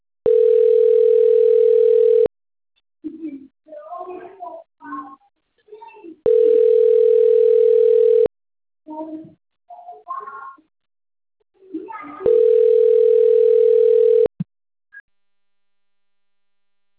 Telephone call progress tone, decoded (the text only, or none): ringback tone